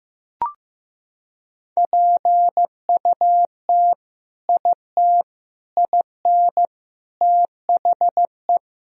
Morse code, heard as a keyed tone, about 15 wpm, 700 Hz. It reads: PUT IT IN THE
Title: Sentences from Top 1000 Words - Review Part 2 - 15wpm